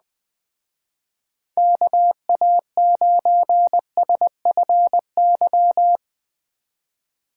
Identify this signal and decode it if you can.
Morse code, heard as a keyed tone, about 20 wpm, 700 Hz. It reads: KA9SFY
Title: Speed Racing - USA 2x3 Call Signs 17wpm